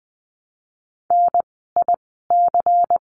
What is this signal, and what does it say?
Morse code, heard as a keyed tone, about 20 wpm, spaced as for 15 wpm, 700 Hz. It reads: NIC
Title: 094 Lesson 16f - Letter c - Words 20wpm